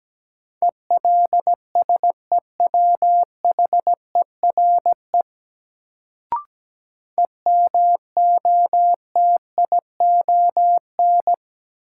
Morse code, heard as a keyed tone, about 17 wpm, 700 Hz. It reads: ELSEWHERE EMOTION
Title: Sentences from Top 1600 Words - New Word Drill - 17wpm